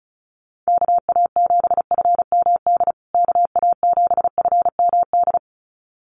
Morse code, heard as a keyed tone, about 34 wpm, 700 Hz. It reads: KA7FMD KA7FMD
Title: USA Call Signs Repeated 2x 35wpm